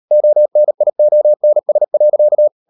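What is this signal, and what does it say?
Morse code, heard as a keyed tone, about 38 wpm, 600 Hz. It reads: ONIONS.
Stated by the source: Morse code practice at 35–39 wpm sending English text of Ken Follett's novel Never, from CW med Gnister.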